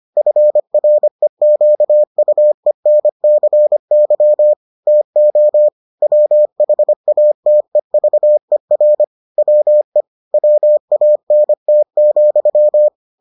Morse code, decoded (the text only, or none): FREQUENCY TO WHATEVER WE WANT,